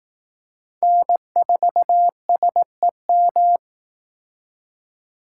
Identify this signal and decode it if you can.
Morse code, heard as a keyed tone, about 18 wpm, 700 Hz. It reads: N4SEM
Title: Speed Racing - USA 1x3 Call Signs 15wpm